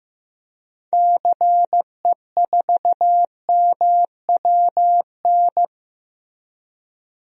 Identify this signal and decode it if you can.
Morse code, heard as a keyed tone, about 15 wpm, 700 Hz. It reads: CE4MWN
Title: DXCC-100 Call Signs - 6 Characters - 15wpm